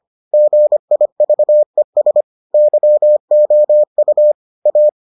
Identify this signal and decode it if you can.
Morse code, heard as a keyed tone, about 25 wpm, 600 Hz. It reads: GIVES YOU A